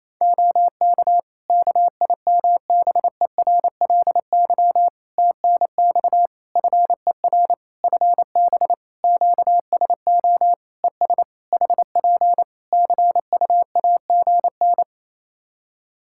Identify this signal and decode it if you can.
Morse code, heard as a keyed tone, about 28 wpm, 700 Hz. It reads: OK KIMBERLY TNX FER FB QSO ES HP CUAGN
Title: Practice QSO - Exchange 4 - Ending - 28wpm